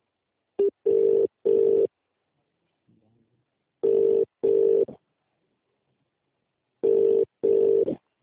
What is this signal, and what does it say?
Ringback tone (UK), 400 + 450 Hz.